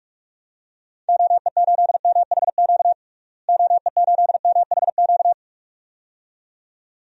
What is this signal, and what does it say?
Morse code, heard as a keyed tone, about 41 wpm, 700 Hz. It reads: OE8MHQ OE8MHQ
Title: DXCC-100 Call Signs - 6 Characters - Repeated 2x - 45wpm